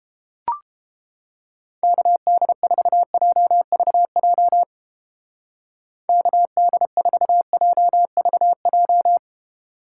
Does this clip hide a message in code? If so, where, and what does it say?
Yes: Morse code from 1.8 to 9.2 s, heard as a keyed tone, about 29 wpm, 700 Hz. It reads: KD4JVJ KD4JVJ